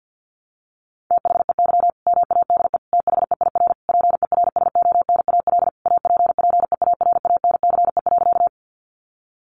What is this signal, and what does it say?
Morse code, heard as a keyed tone, about 50 wpm, 700 Hz. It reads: THEY MADE THEIR PERSONAL APPEARANCE.
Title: Sentences from Top 1100 Words 50wpm